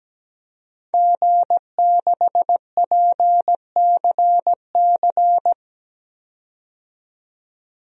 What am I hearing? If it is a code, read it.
Morse code, heard as a keyed tone, about 17 wpm, 700 Hz. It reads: G6PCC
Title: DXCC-100 Call Signs - 5 Characters - 17wpm